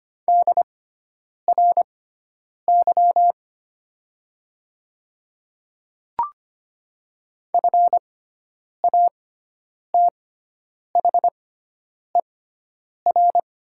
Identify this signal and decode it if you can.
Morse code, heard as a keyed tone, about 25 wpm, spaced as for 9 wpm, 700 Hz. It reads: DRY FATHER
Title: 6x Character Spacing - Top 500 Words - 25wpm